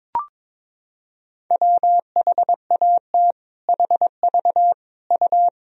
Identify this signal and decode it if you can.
Morse code, heard as a keyed tone, about 22 wpm, 700 Hz. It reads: WHAT HV U
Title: Sentences from Top 100 Words and Abbreviations 22wpm